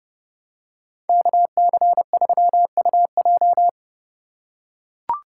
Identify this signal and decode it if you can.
Morse code, heard as a keyed tone, about 28 wpm, 700 Hz. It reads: KC3UJ